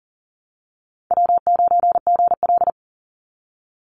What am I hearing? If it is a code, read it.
Morse code, heard as a keyed tone, about 40 wpm, 700 Hz. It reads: W9GL
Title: Call Signs Repeated 2x - 40wpm